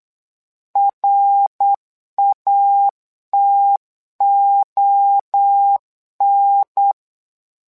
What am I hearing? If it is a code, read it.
Morse code, heard as a keyed tone, about 8 wpm, 800 Hz. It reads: RATON
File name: LEZIONI / 3 MWRO / 3 MWRO 8wpm 2606